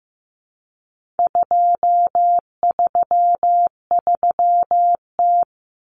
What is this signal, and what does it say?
Morse code, heard as a keyed tone, about 15 wpm, 700 Hz. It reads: 233T
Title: Common Cut Numbers - 4-Digit 15wpm